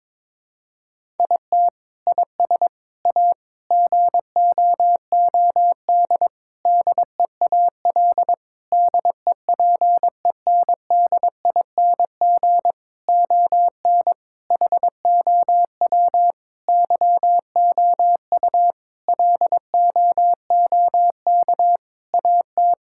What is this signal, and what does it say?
Morse code, heard as a keyed tone, about 21 wpm, 700 Hz. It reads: IT IS A GOOD DEAL DEPENDING ON HOW YOU LOOK AT
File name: Sentences with 7-Word Phrases - 22wpm